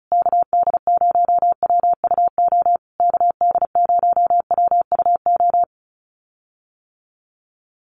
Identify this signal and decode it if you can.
Morse code, heard as a keyed tone, about 34 wpm, 700 Hz. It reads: KD0WUO KD0WUO
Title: USA Call Signs Repeated 3x 35wpm